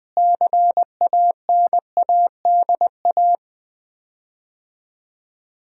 Morse code, heard as a keyed tone, about 20 wpm, 700 Hz. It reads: CANADA
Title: 4x Word-Spacing - Sets of 5 Words - 20wpm